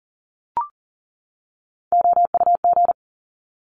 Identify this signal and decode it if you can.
Morse code, heard as a keyed tone, about 40 wpm, 700 Hz. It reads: OUG
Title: Top Three Letter Combinations in Words 40wpm